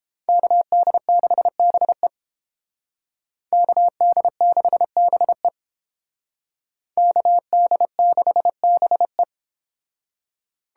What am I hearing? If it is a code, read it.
Morse code, heard as a keyed tone, about 31 wpm, 700 Hz. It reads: KD6BE KD6BE KD6BE